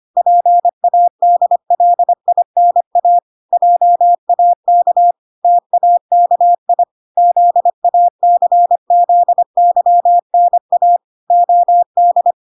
Morse code, heard as a keyed tone, about 25 wpm, 700 Hz. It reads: PADLINA JAK TAKI ZACZYNA OD